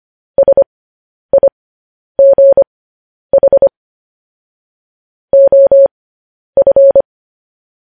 Morse code, heard as a keyed tone, about 25 wpm, spaced as for 10 wpm, 550 Hz. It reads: SIGH OF